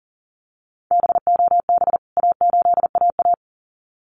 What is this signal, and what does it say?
Morse code, heard as a keyed tone, about 40 wpm, 700 Hz. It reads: BOB A8AA
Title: CWT Exchange 40wpm